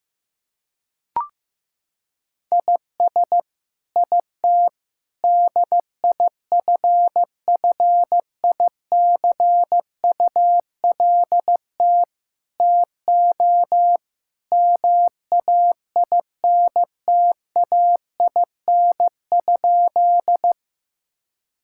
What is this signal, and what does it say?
Morse code, heard as a keyed tone, about 15 wpm, 700 Hz. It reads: IS IT DIFFICULT TO MAINTAIN?
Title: Sentences from Top 1000 Words - Review Part 5 - 15wpm